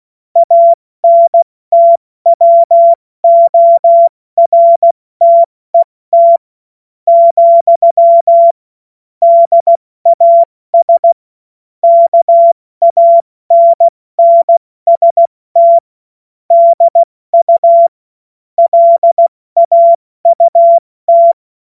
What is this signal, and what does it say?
Morse code, heard as a keyed tone, about 16 wpm, spaced as for 14 wpm, 680 Hz. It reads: ANTWORTET, DAS KANNST DU LAUT
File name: Morsekurs / Witze / Witz 10. Januar 2023 70 F80